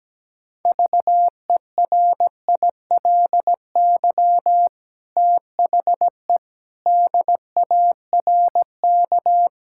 Morse code, heard as a keyed tone, about 17 wpm, 700 Hz. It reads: VERILY THE DARK
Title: Pangram Challenge v2 - 17wpm